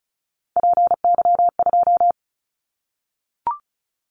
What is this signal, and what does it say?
Morse code, heard as a keyed tone, about 34 wpm, 700 Hz. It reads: PY2
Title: Callsign Prefixes for Top 100 DXCC - 35wpm